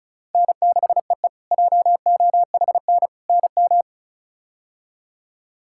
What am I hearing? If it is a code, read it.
Morse code, heard as a keyed tone, about 36 wpm, 700 Hz. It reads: N6EE JOHN NM
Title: K1USN SST QSO Practice - 35wpm